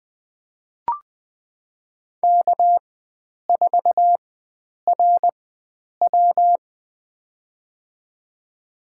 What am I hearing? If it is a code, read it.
Morse code, heard as a keyed tone, about 20 wpm, spaced as for 9 wpm, 700 Hz. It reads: K4RW
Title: USA Call Signs - 4x Character Spacing - 20wpm